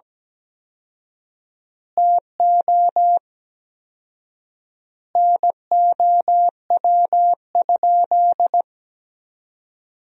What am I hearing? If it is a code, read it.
Morse code, heard as a keyed tone, about 17 wpm, 700 Hz. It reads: TO NOW?
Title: 4x Word-Spacing - Sentences from Top 100 Words - 17wpm